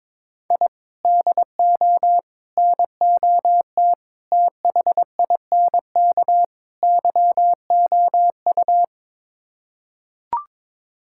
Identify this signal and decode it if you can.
Morse code, heard as a keyed tone, about 22 wpm, 700 Hz. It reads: I DO NOT THINK YOU
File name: practice / 22wpm / Build A Phrase - 5-Word Phrases - 22wpm